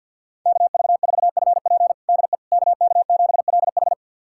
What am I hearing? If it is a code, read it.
Morse code, heard as a keyed tone, about 51 wpm, 700 Hz. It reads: KV4FP DE KK7LH
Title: USA Callsign DE Callsign - 5 Characters - 50wpm